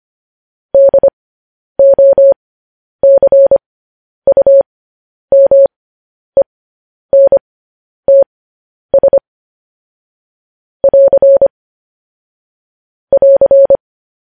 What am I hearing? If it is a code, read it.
Morse code, heard as a keyed tone, about 25 wpm, spaced as for 10 wpm, 550 Hz. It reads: DOCUMENTS <AR> <AR>
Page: news headlines in morse code at 10 wpm